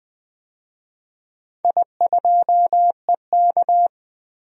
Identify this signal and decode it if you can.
Morse code, heard as a keyed tone, about 21 wpm, 700 Hz. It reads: I2EK
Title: DXCC-100 Call Signs - 4 Characters - 20wpm